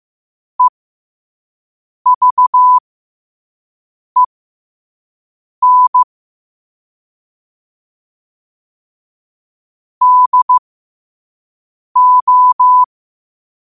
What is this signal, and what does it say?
Morse code, heard as a keyed tone, about 15 wpm, spaced as for 5 wpm, 1000 Hz. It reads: EVEN DO
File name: qotdfiles / 5wpm